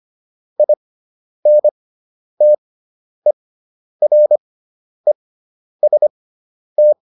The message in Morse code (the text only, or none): INTEREST